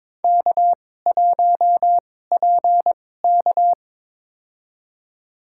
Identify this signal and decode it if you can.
Morse code, heard as a keyed tone, about 22 wpm, spaced as for 16 wpm, 700 Hz. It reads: K1PK
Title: USA Call Signs - 4 Characters - 2x Character Spacing - 22wpm